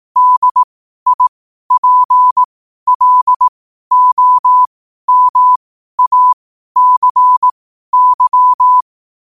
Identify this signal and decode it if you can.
Morse code, heard as a keyed tone, about 18 wpm, spaced as for 12 wpm, 1000 Hz. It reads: DIPLOMACY